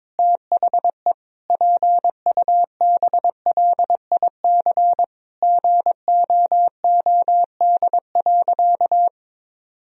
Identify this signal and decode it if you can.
Morse code, heard as a keyed tone, about 22 wpm, 700 Hz. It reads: THE PUBLIC GOOD.
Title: Sentences from Top 1000 Words - Review Part 3 - 22wpm